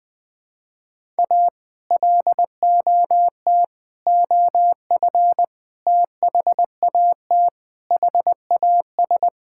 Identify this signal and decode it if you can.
Morse code, heard as a keyed tone, about 20 wpm, 700 Hz. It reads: A LOT OF THAT HAS
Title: Sets of 8 Words - 20wpm